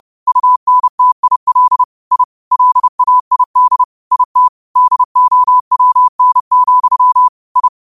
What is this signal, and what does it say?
Morse code, heard as a keyed tone, about 30 wpm, 1000 Hz. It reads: UNTIL I LAID IT DOWN, I
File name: qotdfiles / 30wpm